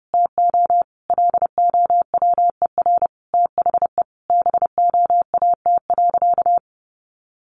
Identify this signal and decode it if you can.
Morse code, heard as a keyed tone, about 28 wpm, 700 Hz. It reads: TO LOWER THE BOAT.
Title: Sentences from Top 700 Words 30wpm